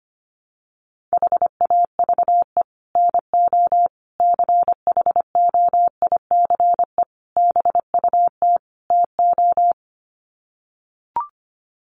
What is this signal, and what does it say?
Morse code, heard as a keyed tone, about 25 wpm, 700 Hz. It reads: HAVE NO CHOICE BUT TO
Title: Sets of 5 Words Encore 25wpm